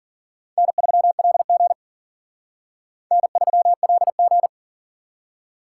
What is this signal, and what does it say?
Morse code, heard as a keyed tone, about 51 wpm, 700 Hz. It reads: N3LG N3LG